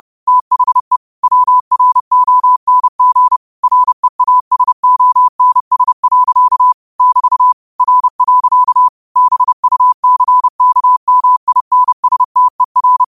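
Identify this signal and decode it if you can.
Morse code, heard as a keyed tone, about 30 wpm, 1000 Hz. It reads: THE WRONG REASONS. <BT> R. BUCKMINSTER
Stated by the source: Morse code quote of the day sent at 30 wpm.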